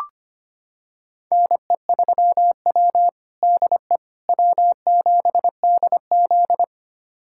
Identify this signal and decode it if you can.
Morse code, heard as a keyed tone, about 25 wpm, 700 Hz. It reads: NE3W DE W7DZ
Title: USA Callsign DE Callsign - 4 Characters - 25wpm